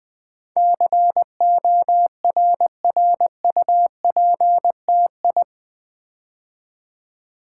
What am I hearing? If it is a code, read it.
Morse code, heard as a keyed tone, about 20 wpm, 700 Hz. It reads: CORRUPTI
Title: Build A Word - 4 Letters At A Time - 10-Letter Words - 20wpm